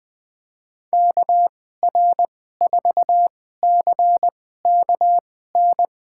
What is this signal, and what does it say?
Morse code, heard as a keyed tone, about 20 wpm, spaced as for 14 wpm, 700 Hz. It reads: KR4CKN
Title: USA Call Signs - 6 Characters - 2x Character Spacing - 20wpm